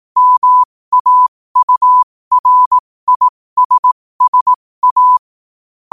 Morse code, heard as a keyed tone, about 18 wpm, spaced as for 15 wpm, 1000 Hz. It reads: MAURISSA
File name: qotdfiles / 15wpm